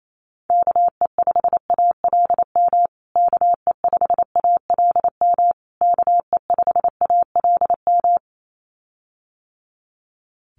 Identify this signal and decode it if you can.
Morse code, heard as a keyed tone, about 28 wpm, 700 Hz. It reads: KE5ALM KE5ALM KE5ALM
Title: USA Call Signs Repeated 3x 28wpm